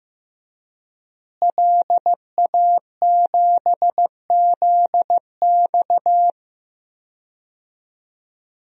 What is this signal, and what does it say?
Morse code, heard as a keyed tone, about 15 wpm, 700 Hz. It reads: LA7ZX
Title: DXCC-100 Call Signs - 5 Characters - Repeated 2x - 15wpm